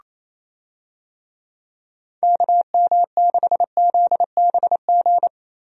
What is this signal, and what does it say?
Morse code, heard as a keyed tone, about 27 wpm, 700 Hz. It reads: KM6ZBG
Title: USA Call Signs - 6 Characters - Repeated 2x - 28wpm